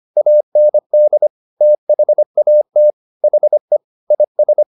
Morse code, heard as a keyed tone, about 25 wpm, 600 Hz. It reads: AND THAT HE IS